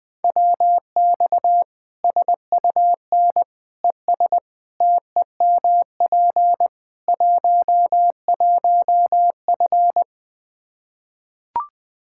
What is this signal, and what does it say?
Morse code, heard as a keyed tone, about 21 wpm, 700 Hz. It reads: WX SUN ES TEMP 11F
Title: Practice QSO - Exchange 2 - Info - 20wpm